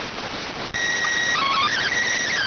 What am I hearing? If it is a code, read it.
SSTV (slow-scan television), PD 290.